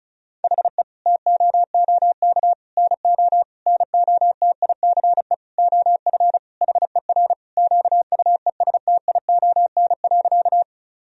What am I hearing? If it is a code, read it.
Morse code, heard as a keyed tone, about 34 wpm, 700 Hz. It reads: HE TOOK NO NOTICE OF HER QUESTION.
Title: Sentences from Top 1000 Words - Review Part 2 - 35wpm